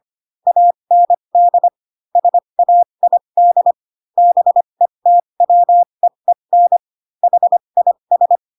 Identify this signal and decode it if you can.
Morse code, heard as a keyed tone, about 25 wpm, spaced as for 22 wpm, 700 Hz. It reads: AND SAID BETWEEN HIS
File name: Pinocchio / Pinocchio ch3 18 25 ik1hge